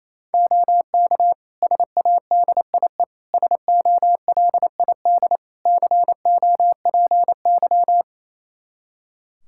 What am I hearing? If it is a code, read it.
Morse code, heard as a keyed tone, about 28 wpm, 700 Hz. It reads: OK SADIE SOLID COPY
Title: Practice QSO - Exchange 3 - More Info - 28wpm